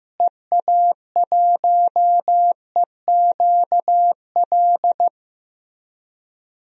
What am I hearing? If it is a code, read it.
Morse code, heard as a keyed tone, about 15 wpm, 700 Hz. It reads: EA1EQL
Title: DXCC-100 Call Signs - 6 Characters - Repeated 2x - 15wpm